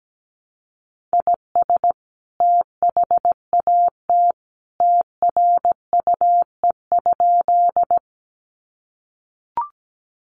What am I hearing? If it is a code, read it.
Morse code, heard as a keyed tone, about 17 wpm, 700 Hz. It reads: IS THAT TRUE?
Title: Sentences from Top 400 Words 17wpm